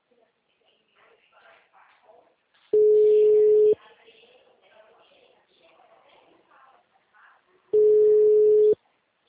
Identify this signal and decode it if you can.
Ringback tone (Europe, 425 Hz), with speech.